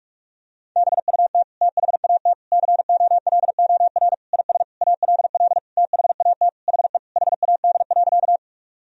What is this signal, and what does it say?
Morse code, heard as a keyed tone, about 46 wpm, 700 Hz. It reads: BUT THAT COLOR IS ALL THAT HE HAD.
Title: Sentences from Top 1000 Words - Review Part 2 - 45wpm